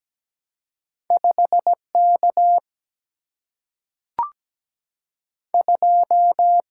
Morse code, heard as a keyed tone, about 17 wpm, 700 Hz. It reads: 5K 2
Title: Single-Letter Single-Number 17wpm